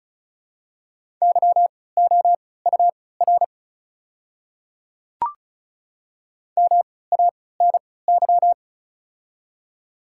Morse code, heard as a keyed tone, about 36 wpm, spaced as for 20 wpm, 700 Hz. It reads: YOUR MANY
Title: 3x Character Spacing - Top 200 Words - 35wpm